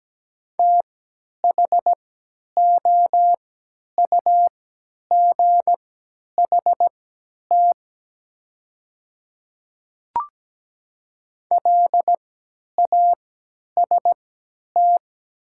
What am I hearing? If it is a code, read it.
Morse code, heard as a keyed tone, about 17 wpm, spaced as for 10 wpm, 700 Hz. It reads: THOUGHT LAST
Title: 3x Character Spacing - Top 300 Words - 17wpm